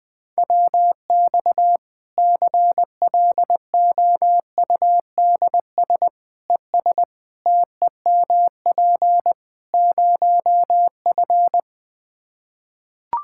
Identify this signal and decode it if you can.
Morse code, heard as a keyed tone, about 20 wpm, 700 Hz. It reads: WX CLOUDS ES TEMP 0F
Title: Practice QSO - Exchange 2 - Info - 20wpm